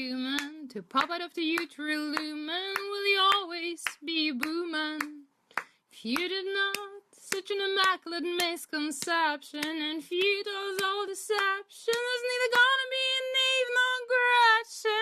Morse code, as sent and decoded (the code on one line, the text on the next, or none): none